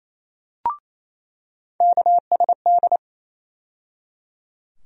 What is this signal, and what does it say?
Morse code, heard as a keyed tone, about 28 wpm, 700 Hz. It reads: KSD
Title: USA Call Sign Postfixes - 3 Letters 28wpm